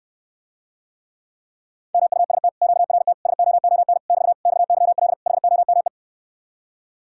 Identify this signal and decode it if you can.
Morse code, heard as a keyed tone, about 69 wpm, 700 Hz. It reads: MUST QRT SOON <BT> XYL HOME